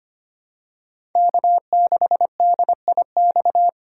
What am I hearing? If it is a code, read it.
Morse code, heard as a keyed tone, about 25 wpm, 700 Hz. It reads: K6DIX
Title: USA Call Signs - 5 Characters - 2x Character Spacing - 25wpm